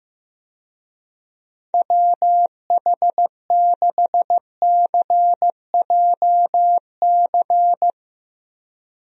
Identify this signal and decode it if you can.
Morse code, heard as a keyed tone, about 15 wpm, 700 Hz. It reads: WH6CJC
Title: USA Call Signs - 6 Characters - Repeated 3x - 15wpm